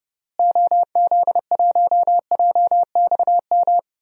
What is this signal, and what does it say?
Morse code, heard as a keyed tone, about 30 wpm, 700 Hz. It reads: OZ1JXM
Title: DXCC-100 Call Signs - 6 Characters - Repeated 2x - 30wpm